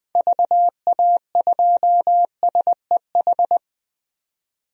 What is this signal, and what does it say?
Morse code, heard as a keyed tone, about 20 wpm, 700 Hz. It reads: VA2SEH
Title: DXCC-100 Call Signs - 6 Characters - 20wpm